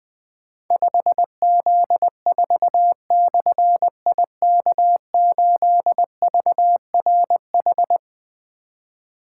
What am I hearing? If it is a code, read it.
Morse code, heard as a keyed tone, about 20 wpm, 700 Hz. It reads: 5Z4/IK8VRH
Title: Difficult Call Signs 20wpm